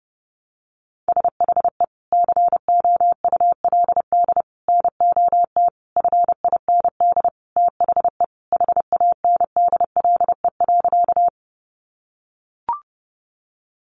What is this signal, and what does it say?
Morse code, heard as a keyed tone, about 30 wpm, 700 Hz. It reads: SHE COULD NOT FIND THE HANDLE.
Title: Sentences from Top 1100 Words 30wpm